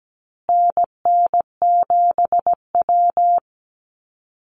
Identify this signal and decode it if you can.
Morse code, heard as a keyed tone, about 17 wpm, 700 Hz. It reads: NN7W
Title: Call Signs Repeated 3x - 17wpm